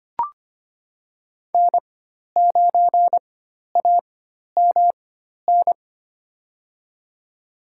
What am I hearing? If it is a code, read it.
Morse code, heard as a keyed tone, about 25 wpm, spaced as for 12 wpm, 700 Hz. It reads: N9AMN
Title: USA Call Signs - 5 Characters - 4x Character Spacing - 25wpm